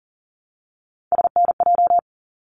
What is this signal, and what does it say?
Morse code, heard as a keyed tone, about 40 wpm, 700 Hz. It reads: SNJ
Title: ARRL Sections 40wpm